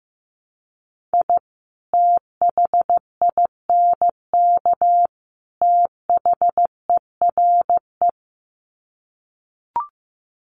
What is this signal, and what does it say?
Morse code, heard as a keyed tone, about 15 wpm, 700 Hz. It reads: I THINK THERE
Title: Sets of 3 Words 15wpm